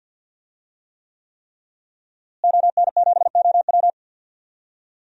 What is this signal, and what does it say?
Morse code, heard as a keyed tone, about 52 wpm, 700 Hz. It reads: ON7OW
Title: DXCC-100 Call Signs - 5 Characters - Repeated 3x - 50wpm